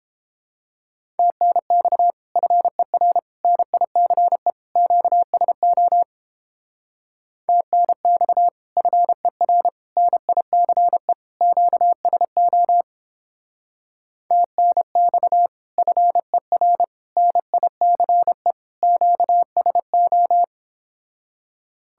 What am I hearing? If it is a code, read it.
Morse code, heard as a keyed tone, about 31 wpm, 700 Hz. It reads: TNX FER NICE QSO TNX FER NICE QSO TNX FER NICE QSO